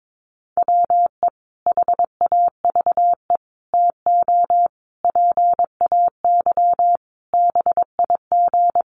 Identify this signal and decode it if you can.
Morse code, heard as a keyed tone, about 22 wpm, 700 Hz. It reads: WE HAVE TO PAY BIG
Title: A Sound Of Thunder 22wpm